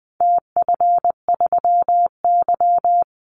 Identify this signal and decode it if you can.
Morse code, heard as a keyed tone, about 20 wpm, 700 Hz. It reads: TF3Y
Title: Call Signs Repeated 3x - 20wpm